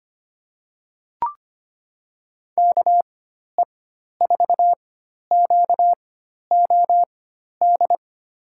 Morse code, heard as a keyed tone, about 25 wpm, spaced as for 12 wpm, 700 Hz. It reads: KE4QOD
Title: USA Call Signs - 4x Character Spacing - 25wpm